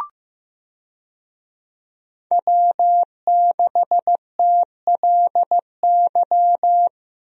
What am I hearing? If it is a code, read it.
Morse code, heard as a keyed tone, about 15 wpm, 700 Hz. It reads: W6TLY
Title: USA Call Signs - 5 Characters - Repeated 3x - 15wpm